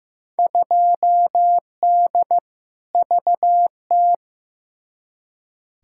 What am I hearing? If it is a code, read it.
Morse code, heard as a keyed tone, about 15 wpm, 700 Hz. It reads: 2D VT
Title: Field Day Exchanges Encore - 15wpm